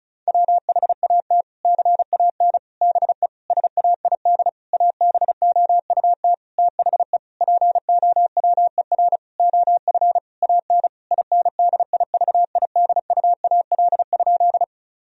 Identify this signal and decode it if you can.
Morse code, heard as a keyed tone, about 34 wpm, 700 Hz. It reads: WHAT CAN BE SAID ABOUT THE POWER OF AN INDIVIDUAL?
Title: Sentences from Top 1000 Words - Review Part 4 - 35wpm